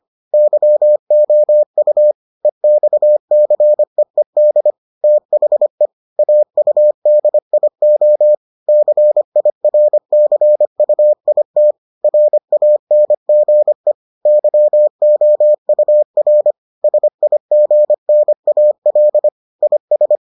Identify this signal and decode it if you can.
Morse code, heard as a keyed tone, about 25 wpm, 600 Hz. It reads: YOU EXCEED THE AUDIO CIRCUIT RANGE YOUR SIGNAL IS